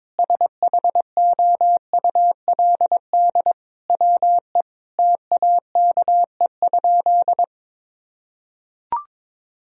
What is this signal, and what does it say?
Morse code, heard as a keyed tone, about 22 wpm, 700 Hz. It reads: SHOULD WE TAKE?
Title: Sentences from Top 1000 Words - Review Part 4 - 22wpm